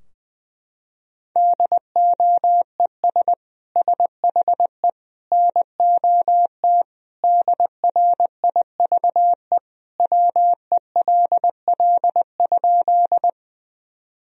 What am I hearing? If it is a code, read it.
Morse code, heard as a keyed tone, about 21 wpm, 700 Hz. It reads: DOES SHE NOT DRIVE WELL?
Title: Sentences from Top 1000 Words - Review Part 2 - 20wpm